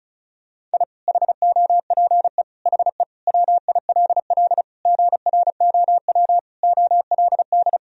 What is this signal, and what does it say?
Morse code, heard as a keyed tone, about 36 wpm, 700 Hz. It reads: I HOPE HE WILL GROW OLD
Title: Sentences from Top 1000 Words - Review Part 1 - 35wpm